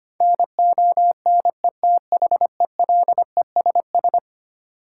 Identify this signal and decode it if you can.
Morse code, heard as a keyed tone, about 25 wpm, 700 Hz. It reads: NONETHELESS
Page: morse-code-ninja-practice-25wpm